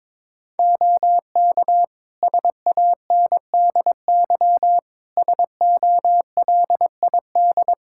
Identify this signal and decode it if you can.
Morse code, heard as a keyed tone, about 22 wpm, 700 Hz. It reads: OK SANDY SOLID
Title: Practice QSO - Exchange 3 - More Info - 22wpm